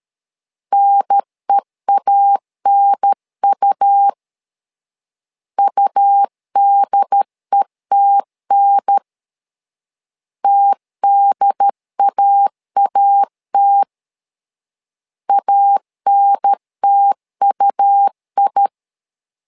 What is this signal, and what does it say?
Morse code, heard as a keyed tone, about 13 wpm, 800 Hz. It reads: NEANU UDETN TDAAT ANTUI
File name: LEZIONI / 2 UHDN / 2 UHDN 12wpm 1456